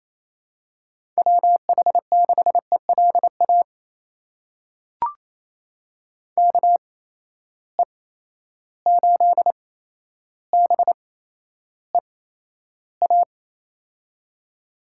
Morse code, heard as a keyed tone, about 28 wpm, spaced as for 12 wpm, 700 Hz. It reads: WH6ELA KE8BEA
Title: USA Call Signs - 6 Characters - 8x Character Spacing - 28wpm